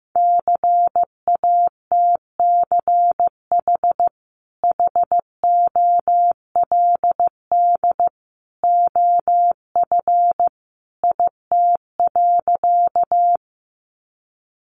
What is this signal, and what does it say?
Morse code, heard as a keyed tone, about 14 wpm, 700 Hz. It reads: CATCH HOLD OF IT.
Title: Sentences from Top 1500 Words 15wpm